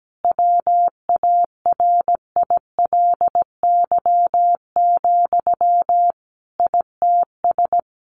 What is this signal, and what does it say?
Morse code, heard as a keyed tone, about 17 wpm, 700 Hz. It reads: WARILY, ITS
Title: A Sound Of Thunder 17wpm